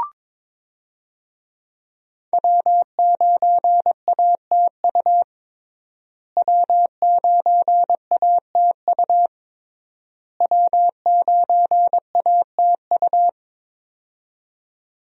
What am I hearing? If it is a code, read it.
Morse code, heard as a keyed tone, about 22 wpm, 700 Hz. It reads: W9ATU W9ATU W9ATU
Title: USA Call Signs - 5 Characters - Repeated 3x - 22wpm